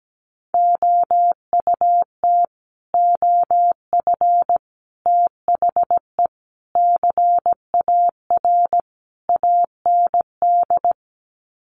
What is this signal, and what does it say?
Morse code, heard as a keyed tone, about 17 wpm, 700 Hz. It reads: OUT OF THE CAR AND
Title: Sets of 5 Words Encore 17wpm